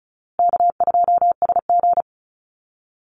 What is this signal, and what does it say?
Morse code, heard as a keyed tone, about 34 wpm, 700 Hz. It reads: K2SG
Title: Easy Call Signs Repeated 3x - 35wpm